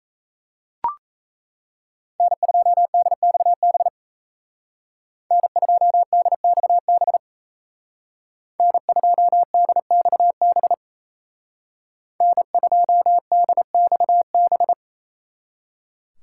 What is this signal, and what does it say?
Morse code, heard as a keyed tone, about 37 wpm, 700 Hz. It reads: N2DXB N2DXB N2DXB N2DXB